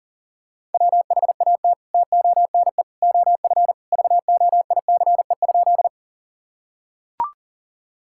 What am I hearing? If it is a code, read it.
Morse code, heard as a keyed tone, about 37 wpm, 700 Hz. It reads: WHAT TONE OF VOICE?
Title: Sentences from Top 1600 Words 40wpm